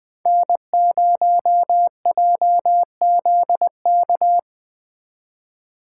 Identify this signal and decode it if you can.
Morse code, heard as a keyed tone, about 22 wpm, 700 Hz. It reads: N0JZK
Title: Speed Racing - USA 1x3 Call Signs 20wpm